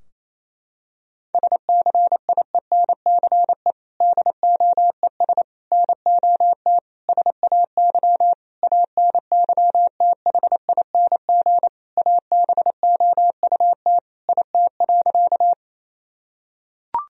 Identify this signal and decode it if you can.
Morse code, heard as a keyed tone, about 28 wpm, 700 Hz. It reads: SCIENCE DOES NOT SAY ANYTHING ABOUT IT.
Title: Sentences from Top 1000 Words - Review Part 2 - 28wpm